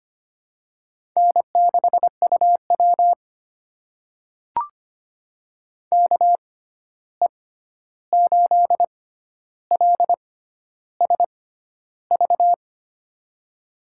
Morse code, heard as a keyed tone, about 25 wpm, spaced as for 11 wpm, 700 Hz. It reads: N6UW KE8LSV
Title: USA Call Signs - 6x Character Spacing - 25wpm